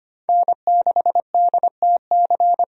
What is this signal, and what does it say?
Morse code, heard as a keyed tone, about 25 wpm, 700 Hz. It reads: N6DTC
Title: USA Call Signs - 5 Characters - 25wpm